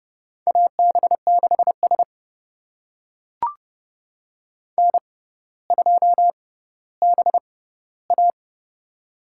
Morse code, heard as a keyed tone, about 28 wpm, spaced as for 15 wpm, 690 Hz. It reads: AB6S N2BA
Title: USA Call Signs - 4 Characters - 6x Character Spacing - 30wpm